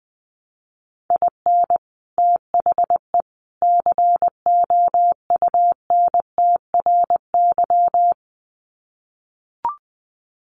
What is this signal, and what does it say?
Morse code, heard as a keyed tone, about 20 wpm, 700 Hz. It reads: IN THE COUNTRY
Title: Sets of 3 Words 20wpm